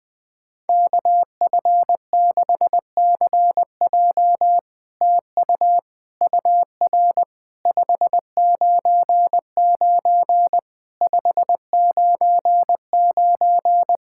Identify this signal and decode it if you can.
Morse code, heard as a keyed tone, about 20 wpm, 700 Hz. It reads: KF6CJ TU UR 599 599
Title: SOTA QSO Practice 20wpm